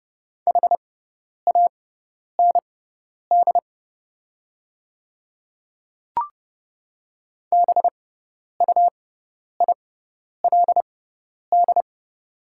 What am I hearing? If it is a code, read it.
Morse code, heard as a keyed tone, about 30 wpm, spaced as for 10 wpm, 700 Hz. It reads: HAND BUILD
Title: 6x Character Spacing - Top 200 Words - 30wpm